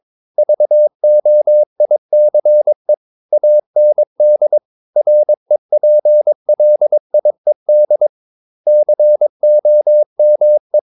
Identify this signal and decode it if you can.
Morse code, heard as a keyed tone, about 22 wpm, 600 Hz. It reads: VOICE AND REPLIED COME
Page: Romanen Never af Ken Follet på engelsk.